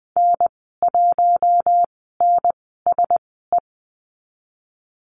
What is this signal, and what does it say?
Morse code, heard as a keyed tone, about 20 wpm, spaced as for 14 wpm, 700 Hz. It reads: N1NSE
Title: 042 Lesson 8g - Number 1 - Call Signs 20wpm